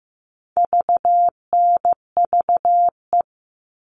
Morse code, heard as a keyed tone, about 15 wpm, 700 Hz. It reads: VNVE
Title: All Cut Numbers - 4-Digit 15wpm